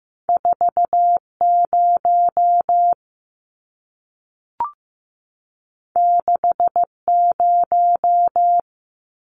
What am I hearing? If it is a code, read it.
Morse code, heard as a keyed tone, about 15 wpm, 700 Hz. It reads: 40 60
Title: Numbers - 2-Digit 15wpm